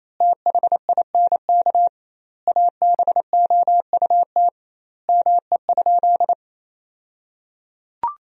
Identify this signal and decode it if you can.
Morse code, heard as a keyed tone, about 28 wpm, 700 Hz. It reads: THINK ABOUT ME?
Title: 2x Word-Spacing - Sentences from Top 200 Words - 28wpm